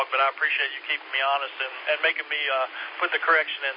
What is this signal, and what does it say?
Emergency Alert System (SAME) tones, heard as harsh buzzing bursts, with speech. It is heard from 0.8 to 3.5 s.